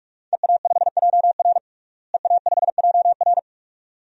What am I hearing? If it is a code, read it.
Morse code, heard as a keyed tone, about 44 wpm, 700 Hz. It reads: EA5JR EA5JR
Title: DXCC-100 Call Signs - 5 Characters - Repeated 2x - 45wpm